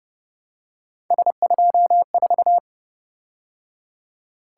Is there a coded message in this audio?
Yes: Morse code from 1.1 to 2.6 s, heard as a keyed tone, about 28 wpm, 700 Hz. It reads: S24